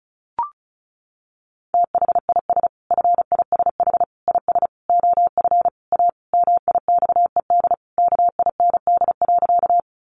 Morse code, heard as a keyed tone, about 34 wpm, 700 Hz. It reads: THIS FISH IS OF A MIXED KIND.
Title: Sentences from Top 1500 Words 35wpm